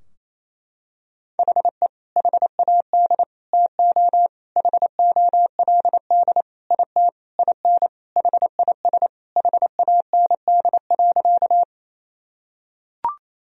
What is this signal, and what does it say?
Morse code, heard as a keyed tone, about 28 wpm, 700 Hz. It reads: HE HAD TO HOLD IT IN HIS HAND.
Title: Sentences from Top 1000 Words - Review Part 2 - 28wpm